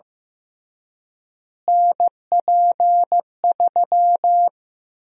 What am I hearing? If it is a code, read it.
Morse code, heard as a keyed tone, about 15 wpm, 700 Hz. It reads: NP3
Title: USA Call Sign Prefixes - Repeated 2x - 15wpm